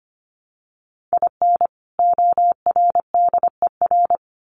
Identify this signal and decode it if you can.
Morse code, heard as a keyed tone, about 25 wpm, 700 Hz. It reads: IN ORDER
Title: Sets of 2 Words 25wpm